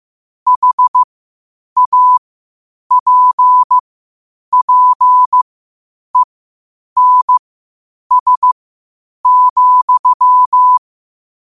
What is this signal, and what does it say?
Morse code, heard as a keyed tone, about 15 wpm, spaced as for 8 wpm, 1000 Hz. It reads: HAPPENS,
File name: qotdfiles / 8wpm